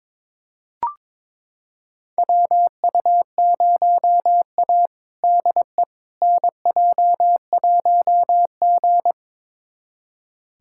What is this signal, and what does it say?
Morse code, heard as a keyed tone, about 22 wpm, 700 Hz. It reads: WU0A DE NJ1G
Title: USA Callsign DE Callsign - 4 Characters - 22wpm